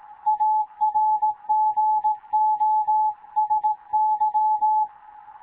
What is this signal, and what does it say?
Morse code, heard as a keyed tone, about 17 wpm, 860 Hz. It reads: ARGOSY
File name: Morze-HA2MN / hangp / 2 11 GYAKORLAT